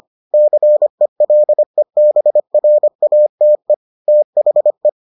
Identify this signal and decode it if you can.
Morse code, heard as a keyed tone, about 25 wpm, 600 Hz. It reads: CELEBRATE THE